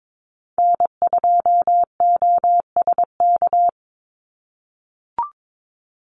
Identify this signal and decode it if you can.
Morse code, heard as a keyed tone, about 22 wpm, 700 Hz. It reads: N2OSK
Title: USA Call Signs 22wpm